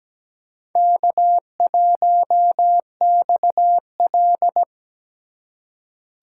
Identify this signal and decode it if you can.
Morse code, heard as a keyed tone, about 17 wpm, 700 Hz. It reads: K1XL
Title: USA Call Signs - 4 Characters - 2x Character Spacing - 17wpm